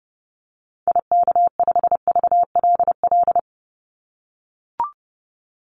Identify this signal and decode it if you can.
Morse code, heard as a keyed tone, about 30 wpm, 700 Hz. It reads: IK5VLL
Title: Call Signs - 30wpm